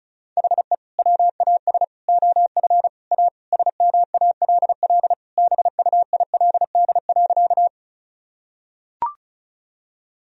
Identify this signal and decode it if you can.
Morse code, heard as a keyed tone, about 34 wpm, 700 Hz. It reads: HE WAS OF A SMALL BUILD.
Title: Sentences from Top 1000 Words - Review Part 1 - 35wpm